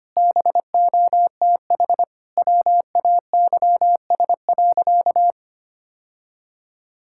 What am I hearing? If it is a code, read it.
Morse code, heard as a keyed tone, about 25 wpm, 700 Hz. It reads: BOTH WAYS.
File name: practice / 25wpm / Sentences with 2-Word Phrases - 25wpm